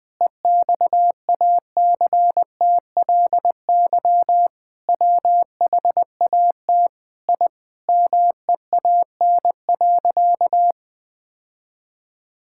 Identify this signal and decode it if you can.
Morse code, heard as a keyed tone, about 20 wpm, 700 Hz. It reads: EXACTLY WHAT I MEAN.
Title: Sentences from Top 1000 Words - Review Part 6 - 20wpm